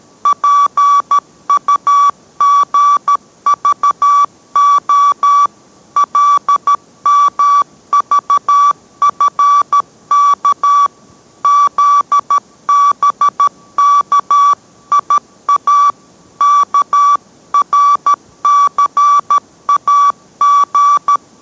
Morse code, heard as a keyed tone, about 13 wpm, 1150 Hz. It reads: PUGVO LMVFK ZBKIA KRCAG